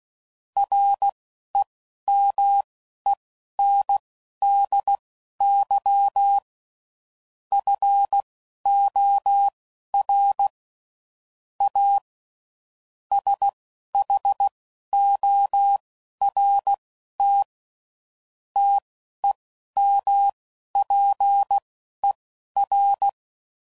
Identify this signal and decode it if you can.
Morse code, heard as a keyed tone, about 16 wpm, spaced as for 11 wpm, 800 Hz. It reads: REMENDY FOR A SHORT TEMPER